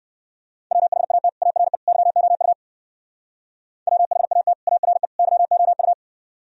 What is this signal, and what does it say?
Morse code, heard as a keyed tone, about 59 wpm, 700 Hz. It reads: WHAT ARE YOU WHAT ARE YOU